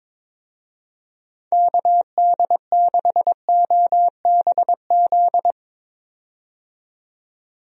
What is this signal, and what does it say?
Morse code, heard as a keyed tone, about 22 wpm, 700 Hz. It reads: KD6OBZ
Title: USA Call Signs - 6 Characters - 22wpm